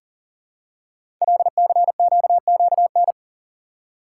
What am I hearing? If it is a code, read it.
Morse code, heard as a keyed tone, about 40 wpm, 700 Hz. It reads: LCQQN
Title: Code Groups - 5 Letters - 40wpm